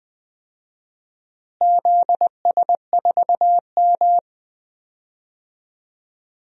Morse code, heard as a keyed tone, about 20 wpm, 700 Hz. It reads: ZS4M
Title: DXCC-100 Call Signs - 4 Characters - Repeated 2x - 20wpm